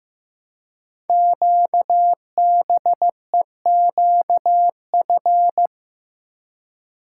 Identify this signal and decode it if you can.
Morse code, heard as a keyed tone, about 15 wpm, 690 Hz. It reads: QBEQF
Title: Code Groups - 5 Letters - 15wpm